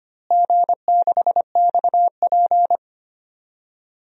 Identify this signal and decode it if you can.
Morse code, heard as a keyed tone, about 25 wpm, 700 Hz. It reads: G6XP
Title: DXCC-100 Call Signs - 4 Characters - Repeated 3x - 25wpm